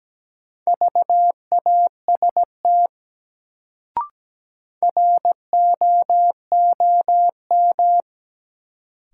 Morse code, heard as a keyed tone, about 17 wpm, 700 Hz. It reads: VAST ROOM
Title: Sets of 4 Words - New Word Drill - 17wpm